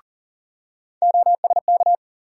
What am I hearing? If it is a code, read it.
Morse code, heard as a keyed tone, about 40 wpm, 700 Hz. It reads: OSK
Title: USA Call Sign Postfixes - 3 Letters 40wpm